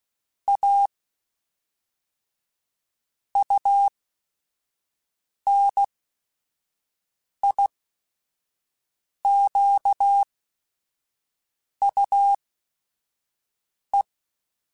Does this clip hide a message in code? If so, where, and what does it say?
Yes: Morse code from 0.5 to 14.0 s, heard as a keyed tone, about 16 wpm, spaced as for 5 wpm, 800 Hz. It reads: A UNIQUE